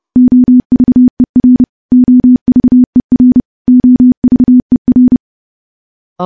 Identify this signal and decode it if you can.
Morse code, heard as a keyed tone, about 30 wpm, 250 Hz. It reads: OVER OVER OVER
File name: morse / morse-training-30WPM